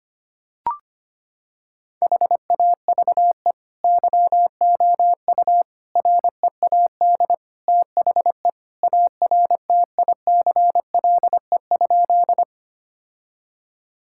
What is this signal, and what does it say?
Morse code, heard as a keyed tone, about 25 wpm, 700 Hz. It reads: HAVE YOU READ THE ARTICLE?
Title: Sentences from Top 1000 Words - Review Part 5 - 25wpm